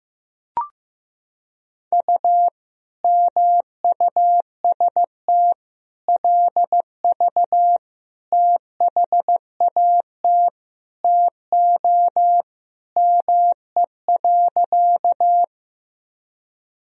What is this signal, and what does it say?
Morse code, heard as a keyed tone, about 15 wpm, 700 Hz. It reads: U MUST LV THAT TO ME.
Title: Sentences from Top 400 Words and Abbreviations 15wpm